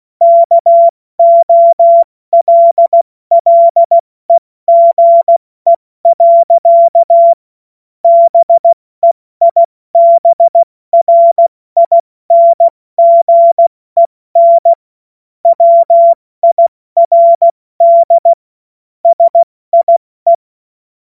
Morse code, heard as a keyed tone, about 16 wpm, 680 Hz. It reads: KOLLEGE. BEIBRINGEN WIRD SIE
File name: Morsekurs / Witze / Witz 16 April 2024 70 F80